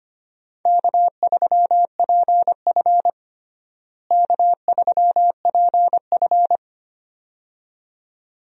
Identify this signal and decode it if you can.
Morse code, heard as a keyed tone, about 25 wpm, 700 Hz. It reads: K3PF K3PF
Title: USA Call Signs - 4 Characters - Repeated 2x - 25wpm